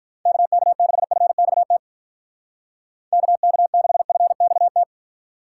Morse code, heard as a keyed tone, about 45 wpm, 700 Hz. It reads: KK6FXT KK6FXT